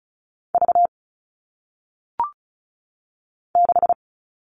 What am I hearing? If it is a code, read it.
Morse code, heard as a keyed tone, about 36 wpm, 700 Hz. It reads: V 6
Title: All Cut Numbers - 1-Digit 35wpm